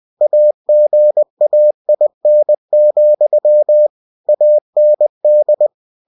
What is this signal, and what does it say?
Morse code, heard as a keyed tone, about 20 wpm, 600 Hz. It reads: AGAIN, AND
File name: cw-books / winnie-the-pooh / winnie-20wpm-page-0023